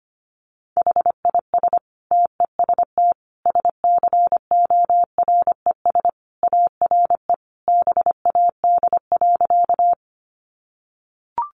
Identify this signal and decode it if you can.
Morse code, heard as a keyed tone, about 25 wpm, 700 Hz. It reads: HIS TEST SCORES ARE BAD.
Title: Additional Easy Phrases 25wpm